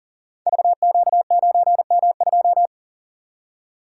Morse code, heard as a keyed tone, about 40 wpm, 700 Hz. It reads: VQ9M2
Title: Code Groups - 5 Letter:Numbers - 40wpm